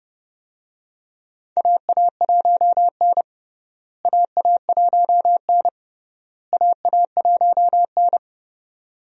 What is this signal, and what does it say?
Morse code, heard as a keyed tone, about 32 wpm, 700 Hz. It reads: AA1N AA1N AA1N
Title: USA Call Signs - 4 Characters - Repeated 3x - 30wpm